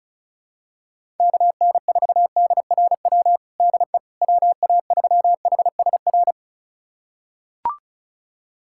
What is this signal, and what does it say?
Morse code, heard as a keyed tone, about 34 wpm, 700 Hz. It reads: KN4DRW DE WA3HSR
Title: USA Callsign DE Callsign 35wpm